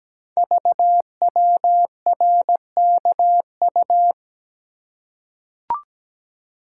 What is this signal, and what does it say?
Morse code, heard as a keyed tone, about 17 wpm, 700 Hz. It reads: VWRKU
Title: Code Groups - 5 Letter:Numbers - 17wpm